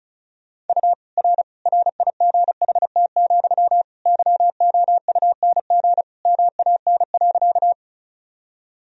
Morse code, heard as a keyed tone, about 36 wpm, 700 Hz. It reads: U R RIGHT, YOUNG MAN.
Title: Sentences from Top 400 Words and Abbreviations 35wpm